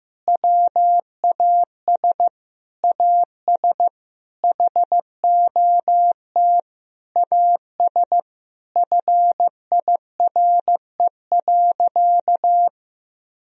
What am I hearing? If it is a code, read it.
Morse code, heard as a keyed tone, about 16 wpm, 700 Hz. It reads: WAS AS HOT AS FIRE.
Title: Sentences from Top 1000 Words - Review Part 3 - 15wpm